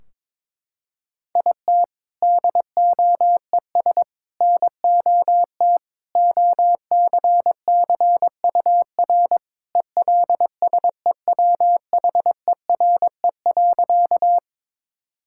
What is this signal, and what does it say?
Morse code, heard as a keyed tone, about 22 wpm, 700 Hz. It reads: IT DOES NOT OCCUR ELSEWHERE.
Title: Sentences from Top 1600 Words 22wpm